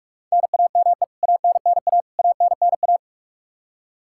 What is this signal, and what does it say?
Morse code, heard as a keyed tone, about 46 wpm, 700 Hz. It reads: NAME ANNA ANNA
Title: Practice QSO - Exchange 1 - Beginning - 45wpm